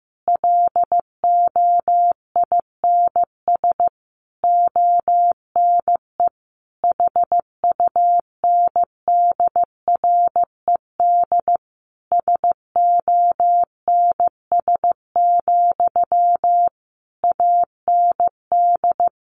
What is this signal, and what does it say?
Morse code, heard as a keyed tone, about 15 wpm, 700 Hz. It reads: LOINS ONE HUNDRED SONS, AND
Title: A Sound Of Thunder 15wpm